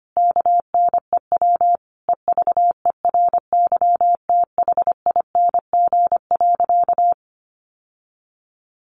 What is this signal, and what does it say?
Morse code, heard as a keyed tone, about 25 wpm, 700 Hz. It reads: KNEW EVERYTHING.
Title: Sentences from Top 700 Words 25wpm